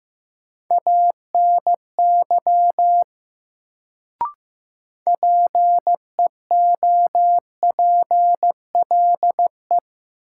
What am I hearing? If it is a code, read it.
Morse code, heard as a keyed tone, about 15 wpm, 700 Hz. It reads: ANY PEOPLE
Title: Sentences from Top 200 Words - New Word Drill - 15wpm